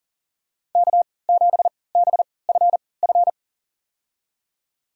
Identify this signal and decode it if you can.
Morse code, heard as a keyed tone, about 37 wpm, spaced as for 22 wpm, 700 Hz. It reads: K7BFF
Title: USA Call Signs - 5 Characters - 3x Character Spacing - 40wpm